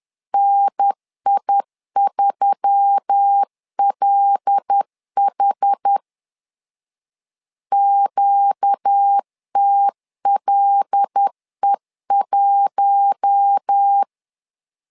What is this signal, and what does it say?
Morse code, heard as a keyed tone, about 10 wpm, 800 Hz. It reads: NI3LH QTLE1
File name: LEZIONI / 5 LBVC / 5 L 10wpm 2259